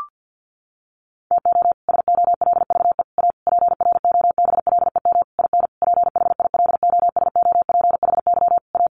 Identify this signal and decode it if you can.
Morse code, heard as a keyed tone, about 50 wpm, 700 Hz. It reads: TO SOLVE A PROBLEM IN PHILOSOPHY A
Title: Sentences from Top 1400 Words 50wpm